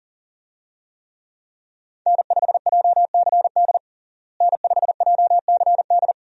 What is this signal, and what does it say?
Morse code, heard as a keyed tone, about 40 wpm, 700 Hz. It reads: N5JCD N5JCD
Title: USA Call Signs - 5 Characters - Repeated 2x - 40wpm